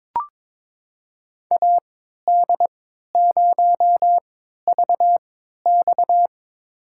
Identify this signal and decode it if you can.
Morse code, heard as a keyed tone, about 22 wpm, spaced as for 13 wpm, 700 Hz. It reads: AD0VX
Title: USA Call Signs - 5 Characters - 3x Character Spacing - 22wpm